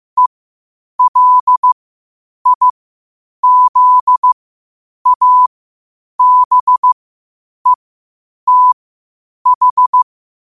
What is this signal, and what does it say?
Morse code, heard as a keyed tone, about 15 wpm, spaced as for 8 wpm, 1000 Hz. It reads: ELIZABETH
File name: qotdfiles / 8wpm